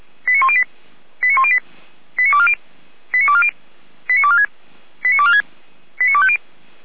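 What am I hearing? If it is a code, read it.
Five-tone selective call (ZVEI), heard as quick runs of five tones, tones of about 70 ms. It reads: 89108 89108 89240 89240 89246 89246 89250
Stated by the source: ZVEI five-tone selective call received on an SDR, from Feuerwehr Orlamünde.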